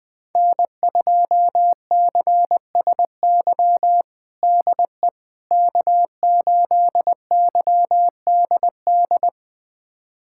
Morse code, heard as a keyed tone, about 21 wpm, 700 Hz. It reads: N2CSY DE K8YDD
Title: USA Callsign DE Callsign - 5 Characters - 20wpm